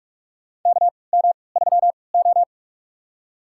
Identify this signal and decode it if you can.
Morse code, heard as a keyed tone, about 46 wpm, spaced as for 25 wpm, 690 Hz. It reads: KM3O
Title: USA Call Signs - 4 Characters - 3x Character Spacing - 45wpm